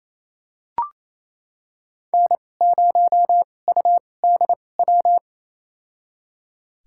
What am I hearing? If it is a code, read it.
Morse code, heard as a keyed tone, about 28 wpm, spaced as for 20 wpm, 700 Hz. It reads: N0UDW
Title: USA Call Signs - 5 Characters - 2x Character Spacing - 28wpm